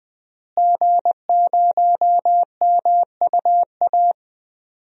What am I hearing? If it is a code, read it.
Morse code, heard as a keyed tone, about 20 wpm, 700 Hz. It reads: G0MUA
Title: DXCC-100 Call Signs - 5 Characters - Repeated 3x - 20wpm